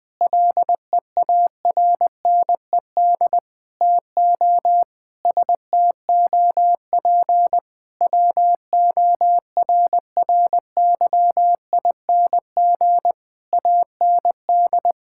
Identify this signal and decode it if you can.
Morse code, heard as a keyed tone, about 21 wpm, 700 Hz. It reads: LEARNED TO STOP WORRYING AND
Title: Sentences with 8-Word Phrases - Spoken Beginning - 20wpm